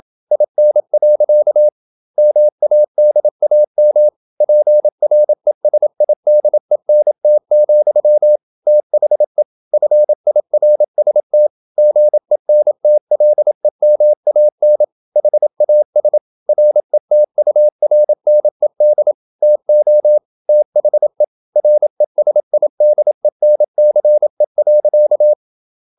Morse code, heard as a keyed tone, about 27 wpm, 600 Hz. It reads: IN. MADAM PRESIDENT, THE FIRST GENTLEMAN HAS RETURNED TO THE RESIDENCE.
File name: Never 0027wpm